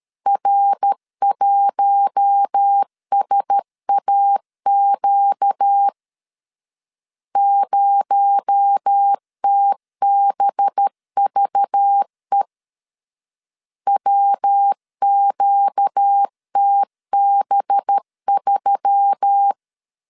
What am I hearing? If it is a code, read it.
Morse code, heard as a keyed tone, about 13 wpm, 800 Hz. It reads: R1SAQ 0TBVE WQTB3
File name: LEZIONI / 6 PKX0 / 6 PKX0 12wpm 2231